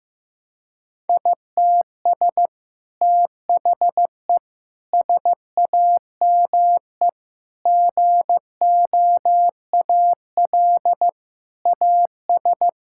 Morse code, heard as a keyed tone, about 15 wpm, 700 Hz. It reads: ITS THE SAME GOAL AS
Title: Sentences from Top 1000 Words - Review Part 5 - 15wpm